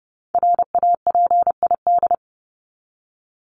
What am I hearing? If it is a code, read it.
Morse code, heard as a keyed tone, about 30 wpm, 700 Hz. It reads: RAPID
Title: LY Suffix - Top 100 Words with Base Words 30wpm